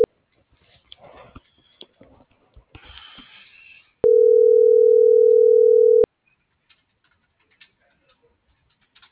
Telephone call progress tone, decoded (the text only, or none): ringback tone